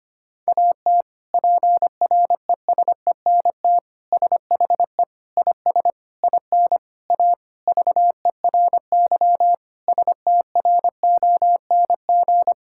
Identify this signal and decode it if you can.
Morse code, heard as a keyed tone, about 25 wpm, 700 Hz. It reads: AT PRESENT SHE IS IN A VERY STRONG
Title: Sentences from Top 1000 Words - Review Part 5 - 25wpm